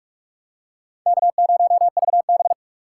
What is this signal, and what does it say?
Morse code, heard as a keyed tone, about 43 wpm, 700 Hz. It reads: K0VB
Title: USA Call Signs - 4x Character Spacing - 45wpm